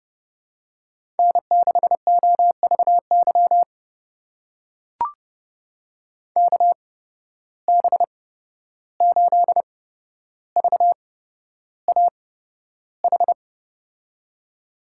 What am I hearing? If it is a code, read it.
Morse code, heard as a keyed tone, about 31 wpm, spaced as for 12 wpm, 700 Hz. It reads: N6OVY KB8VAH
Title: USA Call Signs - 8x Character Spacing - 30wpm